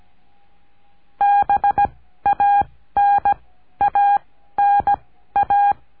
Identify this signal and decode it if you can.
Morse code, heard as a keyed tone, about 17 wpm, spaced as for 12 wpm, 810 Hz. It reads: BANANA
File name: cursos / CURSO DE TELEGRAFIA / Telegrafia - Trilha 14